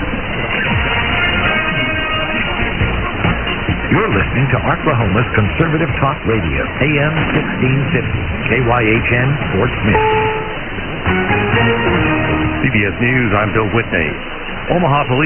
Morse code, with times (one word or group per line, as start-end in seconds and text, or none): none